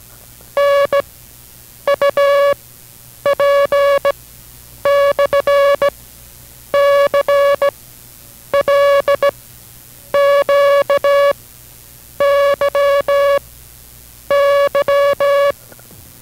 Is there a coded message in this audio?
Yes: Morse code from 0.6 to 15.5 s, heard as a keyed tone, about 17 wpm, 560 Hz. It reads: N U P / C L Q Y Y